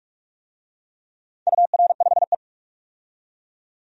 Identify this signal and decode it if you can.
Morse code, heard as a keyed tone, about 43 wpm, 700 Hz. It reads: UR5E
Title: DXCC-100 Call Signs - 4 Characters - 45wpm